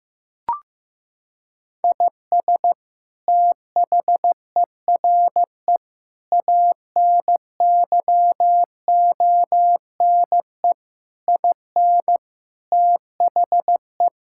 Morse code, heard as a keyed tone, about 15 wpm, 700 Hz. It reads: IS THERE ANYONE IN THE
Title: Sentences from Top 1000 Words - Review Part 5 - 15wpm